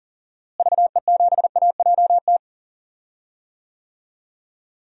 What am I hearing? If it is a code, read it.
Morse code, heard as a keyed tone, about 40 wpm, 700 Hz. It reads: VE7AJT
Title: DXCC-100 Call Signs - 6 Characters - Repeated 2x - 40wpm